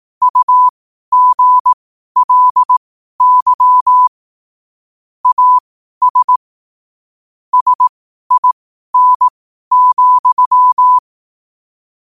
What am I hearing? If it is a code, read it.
Morse code, heard as a keyed tone, about 18 wpm, spaced as for 12 wpm, 1000 Hz. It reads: UGLY AS SIN,
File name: 12wpm